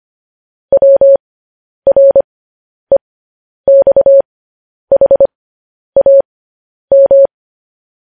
Morse code, heard as a keyed tone, about 25 wpm, spaced as for 10 wpm, 550 Hz. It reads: WREXHAM
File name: podcast / News headlines-1x-25x10